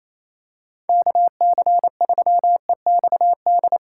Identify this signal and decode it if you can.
Morse code, heard as a keyed tone, about 28 wpm, 700 Hz. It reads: KC3EXD
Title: USA Call Signs - 4x Character Spacing - 28wpm